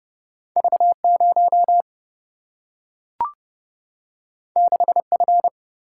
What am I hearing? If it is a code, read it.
Morse code, heard as a keyed tone, about 30 wpm, 700 Hz. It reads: V0 6F
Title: Single-Letter Single-Number 30wpm